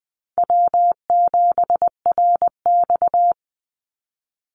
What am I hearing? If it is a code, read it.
Morse code, heard as a keyed tone, about 20 wpm, 700 Hz. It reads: W7RX
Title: Call Signs Repeated 2x - 20wpm